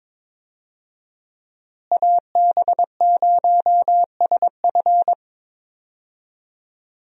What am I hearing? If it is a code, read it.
Morse code, heard as a keyed tone, about 22 wpm, 700 Hz. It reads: AB0SF
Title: USA Call Signs - 5 Characters - Repeated 3x - 22wpm